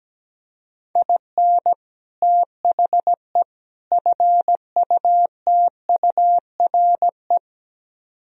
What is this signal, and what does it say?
Morse code, heard as a keyed tone, about 17 wpm, 700 Hz. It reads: IN THE FUTURE
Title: Speed-Racing - Sets of 3 Words 17wpm